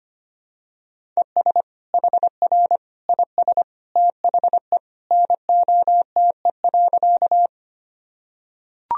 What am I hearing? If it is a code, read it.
Morse code, heard as a keyed tone, about 25 wpm, 700 Hz. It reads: ES HR IS THE NOTE.
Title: Sentences from Top 500 Words and Abbreviations 25wpm